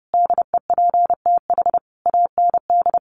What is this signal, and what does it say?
Morse code, heard as a keyed tone, about 30 wpm, 700 Hz. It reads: DEPTH AND
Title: Sentences from Top 1200 Words 30wpm